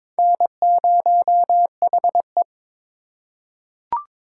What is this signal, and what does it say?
Morse code, heard as a keyed tone, about 22 wpm, 700 Hz. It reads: N0HE
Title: USA Call Signs - 2x Character Spacing - 22wpm